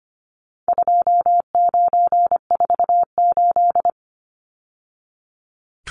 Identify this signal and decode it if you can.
Morse code, heard as a keyed tone, about 25 wpm, 700 Hz. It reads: 2948
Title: Numbers - 4-Digit 25wpm